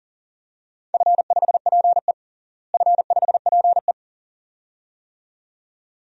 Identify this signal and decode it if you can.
Morse code, heard as a keyed tone, about 40 wpm, 700 Hz. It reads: F5PE F5PE
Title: DXCC-100 Call Signs - 4 Characters - Repeated 2x - 40wpm